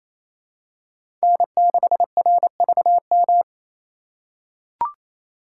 Morse code, heard as a keyed tone, about 28 wpm, 700 Hz. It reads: N6RVM
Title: USA Call Signs - 5 Characters - 2x Character Spacing - 28wpm